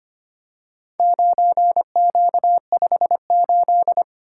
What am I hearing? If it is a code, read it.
Morse code, heard as a keyed tone, about 25 wpm, 700 Hz. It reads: 9Q58
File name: practice / 25wpm / Four Letter-Number Combinations - 25wpm